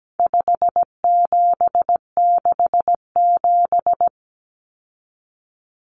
Morse code, heard as a keyed tone, about 17 wpm, 700 Hz. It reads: 5767
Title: Common Cut Numbers - 4-Digit 17wpm